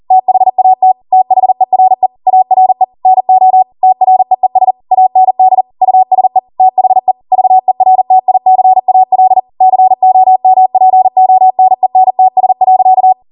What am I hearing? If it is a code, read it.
Morse code, heard as a keyed tone, about 40 wpm, 750 Hz. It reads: THAT THERE ARE NO TREES AND USE THE VERTICAL COMPONENTS.